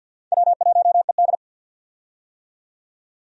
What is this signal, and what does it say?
Morse code, heard as a keyed tone, about 51 wpm, 700 Hz. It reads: W1ED
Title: USA Call Signs - 4 Characters - 50wpm